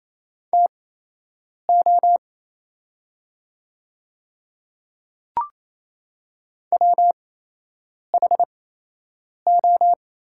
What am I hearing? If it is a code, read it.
Morse code, heard as a keyed tone, about 28 wpm, spaced as for 8 wpm, 700 Hz. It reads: TO WHO
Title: 8x Character Spacing - Top 200 Words - 28wpm